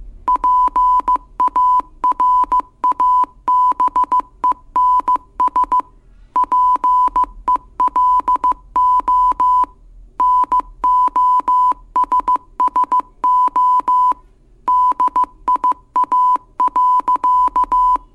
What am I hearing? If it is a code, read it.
Morse code, heard as a keyed tone, about 15 wpm, 1000 Hz. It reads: PARABENS PELO NOSSO DIA.